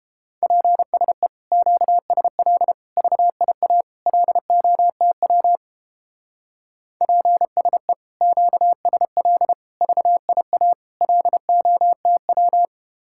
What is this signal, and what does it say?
Morse code, heard as a keyed tone, about 29 wpm, 700 Hz. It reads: PSE QSL VIA LOTW PSE QSL VIA LOTW